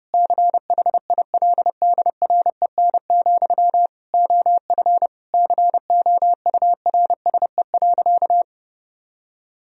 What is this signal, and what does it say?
Morse code, heard as a keyed tone, about 30 wpm, 700 Hz. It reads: CHILDREN, OF COURSE.
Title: Sentences from Top 1000 Words - Review Part 1 - 30wpm